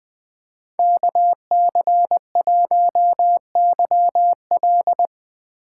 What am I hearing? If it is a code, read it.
Morse code, heard as a keyed tone, about 20 wpm, 700 Hz. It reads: KC1YL
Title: 2023 POTA Activator Call Signs - 20wpm